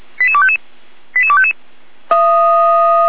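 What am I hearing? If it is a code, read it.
Five-tone selective call (ZVEI), heard as quick runs of five tones, tones of about 80 ms. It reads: 88255 88255
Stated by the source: ZVEI five-tone call received by SDR, from feuerwehr-orlamuende.org fire brigade audio archive.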